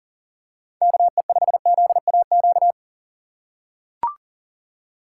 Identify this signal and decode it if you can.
Morse code, heard as a keyed tone, about 40 wpm, 700 Hz. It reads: KE5ZAQ
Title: USA Call Signs - 6 Characters - 3x Character Spacing - 40wpm